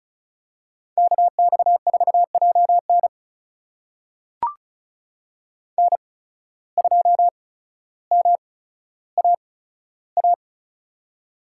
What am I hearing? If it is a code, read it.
Morse code, heard as a keyed tone, about 36 wpm, spaced as for 15 wpm, 700 Hz. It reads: KX4JN N2MAA
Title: USA Call Signs - 5 Characters - 8x Character Spacing - 35wpm